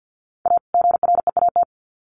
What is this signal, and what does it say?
Morse code, heard as a keyed tone, about 50 wpm, 700 Hz. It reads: A GREAT
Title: Sets of 2 Words 50wpm